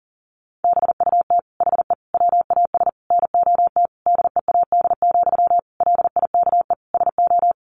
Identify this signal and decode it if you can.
Morse code, heard as a keyed tone, about 40 wpm, 700 Hz. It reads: BUT HE WAS NOT DEAD, LIKE SO
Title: Sentences from Top 1500 Words 40wpm